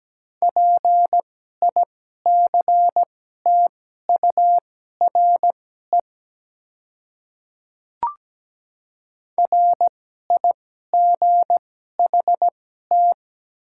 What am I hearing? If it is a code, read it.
Morse code, heard as a keyed tone, about 17 wpm, spaced as for 12 wpm, 700 Hz. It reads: PICTURE RIGHT
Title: 2x Character Spacing - Top 200 Words - 17wpm